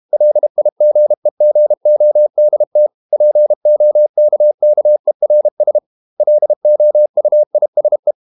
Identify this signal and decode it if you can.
Morse code, heard as a keyed tone, about 32 wpm, 600 Hz. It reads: LIGEGODT POKKERS LOUISE
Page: Ferie 32 wpm